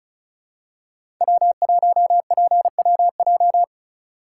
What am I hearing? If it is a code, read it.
Morse code, heard as a keyed tone, about 36 wpm, 700 Hz. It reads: W1PWJ
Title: USA Call Signs - 5 Characters - 2x Character Spacing - 35wpm